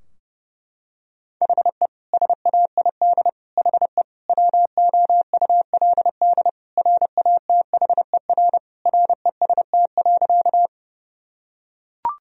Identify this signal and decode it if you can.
Morse code, heard as a keyed tone, about 32 wpm, 700 Hz. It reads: HE SAID HE WOULD RATHER REST.
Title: Sentences from Top 1000 Words - Review Part 4 - 30wpm